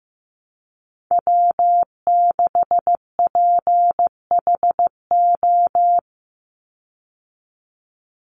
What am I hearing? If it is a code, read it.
Morse code, heard as a keyed tone, about 15 wpm, 700 Hz. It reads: W6PHO
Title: Call Signs Repeated 3x - 15wpm